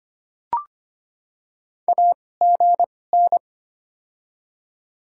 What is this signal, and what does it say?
Morse code, heard as a keyed tone, about 25 wpm, spaced as for 18 wpm, 700 Hz. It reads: AGN
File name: Top 20 QSO Elements - 2x Character Spacing - 25wpm